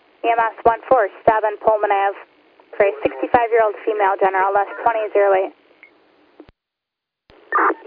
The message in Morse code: none

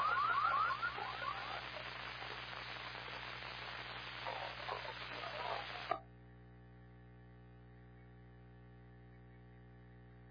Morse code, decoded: none

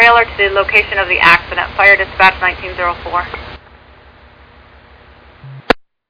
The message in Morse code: none

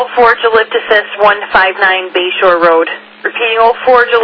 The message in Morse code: none

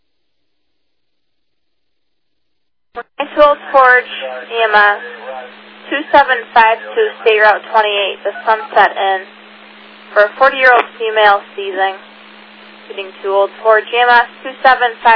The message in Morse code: none

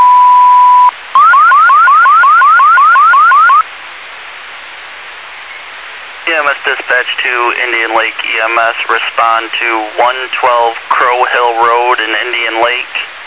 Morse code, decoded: none

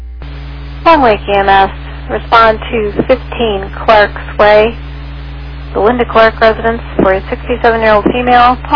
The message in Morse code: none

